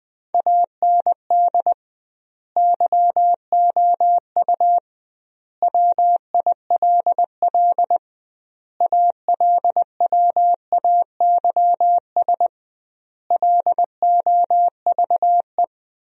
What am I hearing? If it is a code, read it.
Morse code, heard as a keyed tone, about 21 wpm, 700 Hz. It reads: AND YOU WILL ALWAYS LOVE
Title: 2x Word-Spacing - Sentences from Top 400 Words - 20wpm